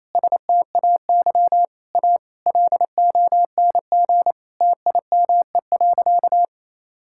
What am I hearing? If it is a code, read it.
Morse code, heard as a keyed tone, about 28 wpm, 700 Hz. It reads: STAY A LONG TIME.
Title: Sentences from Top 1000 Words - Review Part 6 - 28wpm